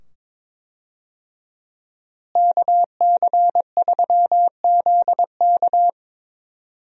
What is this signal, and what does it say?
Morse code, heard as a keyed tone, about 22 wpm, 700 Hz. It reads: KC3ZK
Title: USA Call Signs - 5 Characters - Repeated 2x - 22wpm